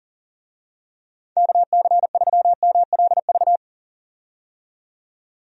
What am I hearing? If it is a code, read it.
Morse code, heard as a keyed tone, about 40 wpm, 700 Hz. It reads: KC3MLV
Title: USA Call Signs - 6 Characters - 40wpm